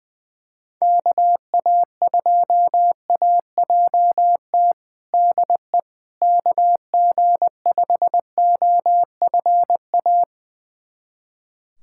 Morse code, heard as a keyed tone, about 20 wpm, 700 Hz. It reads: KA2AJT DE KG5OFA
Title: USA Callsign DE Callsign - 6 Characters - 20wpm